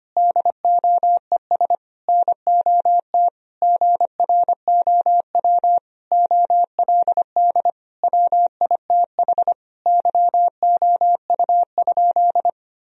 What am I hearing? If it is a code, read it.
Morse code, heard as a keyed tone, about 25 wpm, 700 Hz. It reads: DOES NOT GROW OLD WITH YOU?
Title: Sentences from Top 1000 Words - Review Part 1 - 25wpm